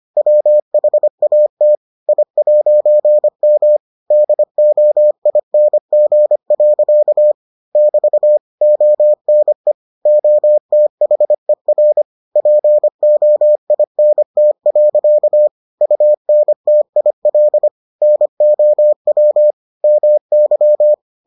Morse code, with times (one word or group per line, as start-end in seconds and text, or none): 0.2-1.8 WHAT
2.1-3.8 I'M
4.1-7.3 DOING.
7.8-8.4 <BT>
8.6-9.7 ONE
10.1-12.0 OTHER
12.4-15.5 POINT.
15.8-17.7 UNTIL
18.0-19.5 NOW
19.8-20.9 MY